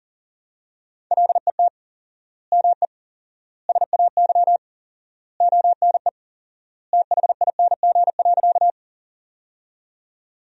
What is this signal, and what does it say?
Morse code, heard as a keyed tone, about 40 wpm, 700 Hz. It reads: LET ME SAY ONE THING.
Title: 4x Word-Spacing - Sentences from Top 300 Words - 40wpm